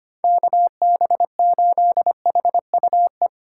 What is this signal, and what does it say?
Morse code, heard as a keyed tone, about 28 wpm, 700 Hz. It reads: KB8HUE
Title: Speed Racing - USA 2x3 Call Signs 25wpm